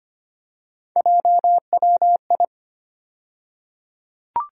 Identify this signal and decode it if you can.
Morse code, heard as a keyed tone, about 25 wpm, 700 Hz. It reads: JWI
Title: USA Call Sign Postfixes - 3 Letters 25wpm